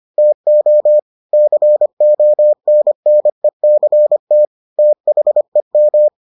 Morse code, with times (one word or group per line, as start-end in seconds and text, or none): 0.2-1.0 TO
1.3-4.5 CONNECT
4.8-6.1 THEM